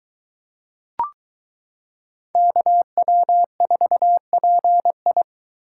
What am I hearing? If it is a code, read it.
Morse code, heard as a keyed tone, about 22 wpm, 700 Hz. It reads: KW4PI